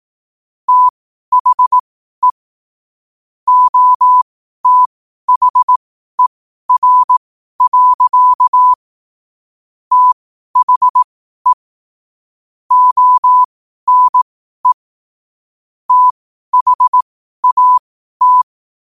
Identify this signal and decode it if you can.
Morse code, heard as a keyed tone, about 18 wpm, spaced as for 12 wpm, 1000 Hz. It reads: THE OTHER. THE ONE THAT